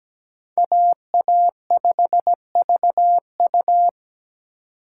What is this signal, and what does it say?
Morse code, heard as a keyed tone, about 17 wpm, 700 Hz. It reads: AA5VU
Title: K1USN SST QSO Practice - 17wpm